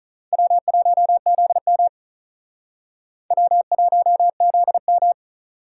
Morse code, heard as a keyed tone, about 45 wpm, 700 Hz. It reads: W1ZM W1ZM